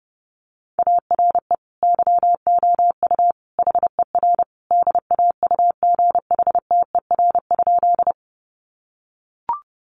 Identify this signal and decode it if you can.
Morse code, heard as a keyed tone, about 30 wpm, 700 Hz. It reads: ARE YOU HER DAUGHTER?
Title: Sentences from Top 1500 Words 30wpm